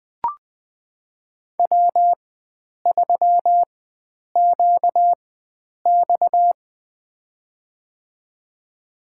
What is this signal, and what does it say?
Morse code, heard as a keyed tone, about 21 wpm, spaced as for 9 wpm, 700 Hz. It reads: W3QX
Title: USA Call Signs - 4 Characters - 4x Character Spacing - 20wpm